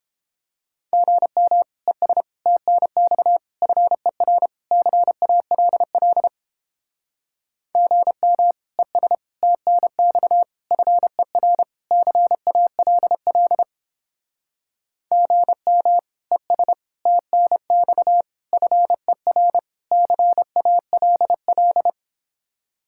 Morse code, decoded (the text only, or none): GM ES TNX FER CALL GM ES TNX FER CALL GM ES TNX FER CALL